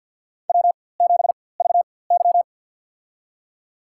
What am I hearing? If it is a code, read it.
Morse code, heard as a keyed tone, about 47 wpm, spaced as for 23 wpm, 700 Hz. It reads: W7VY
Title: USA Call Signs - 4x Character Spacing - 50wpm